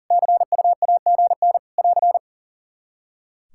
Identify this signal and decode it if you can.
Morse code, heard as a keyed tone, about 40 wpm, 700 Hz. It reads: CUAGN <AR>
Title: Practice QSO - Exchange 4 - Ending - 40wpm